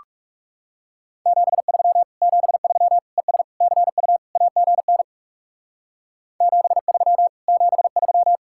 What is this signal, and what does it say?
Morse code, heard as a keyed tone, about 36 wpm, 700 Hz. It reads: 73 73 ES CU AGN 73 73
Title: Speed-Racing Common QSO Phrases Encore - 30wpm